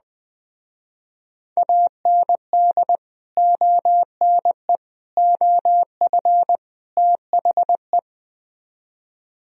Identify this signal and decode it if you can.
Morse code, heard as a keyed tone, about 21 wpm, 700 Hz. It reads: AND ONE OF THE
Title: Speed-Racing - Sets of 4 Words 20wpm